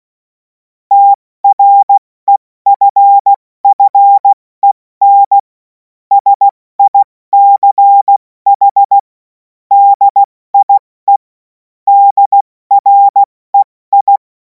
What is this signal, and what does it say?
Morse code, heard as a keyed tone, about 16 wpm, 800 Hz. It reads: TREFFEN SICH DIE DREI
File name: Morsekurs / Witze / Witz 11. August 2026 70 F80